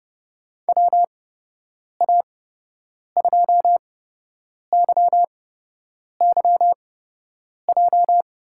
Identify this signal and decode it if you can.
Morse code, heard as a keyed tone, about 31 wpm, spaced as for 8 wpm, 700 Hz. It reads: WA2YYJ
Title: USA Call Signs - 6 Characters - 8x Character Spacing - 30wpm